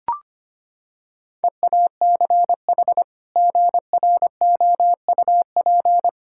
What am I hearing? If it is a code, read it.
Morse code, heard as a keyed tone, about 25 wpm, 700 Hz. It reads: EACH GROUP
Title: Sentences from Top 1000 Words - Review Part 6 - 25wpm